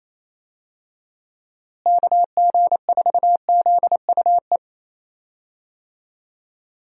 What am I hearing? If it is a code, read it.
Morse code, heard as a keyed tone, about 27 wpm, 700 Hz. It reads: KG4ZUE
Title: USA Call Signs - 6 Characters - Repeated 2x - 28wpm